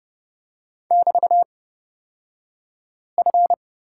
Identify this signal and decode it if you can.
Morse code, heard as a keyed tone, about 31 wpm, spaced as for 5 wpm, 700 Hz. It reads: <BT>F
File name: Lesson 36d - BT - Two Characters 30wpm